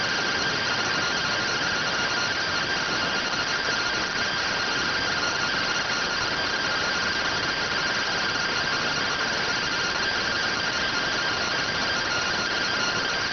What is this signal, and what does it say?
Two-tone teleprinter signal, about 170 Hz shift.